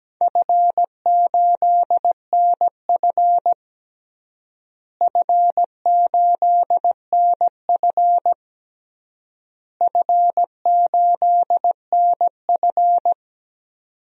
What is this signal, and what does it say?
Morse code, heard as a keyed tone, about 17 wpm, 700 Hz. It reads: F8NF F8NF F8NF
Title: DXCC-100 Call Signs - 4 Characters - Repeated 3x - 17wpm